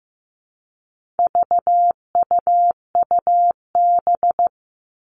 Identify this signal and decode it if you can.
Morse code, heard as a keyed tone, about 15 wpm, 700 Hz. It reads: VUUB
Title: All Cut Numbers - 4-Digit 15wpm